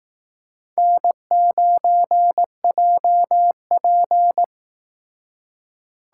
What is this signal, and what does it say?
Morse code, heard as a keyed tone, about 19 wpm, 700 Hz. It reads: N9JP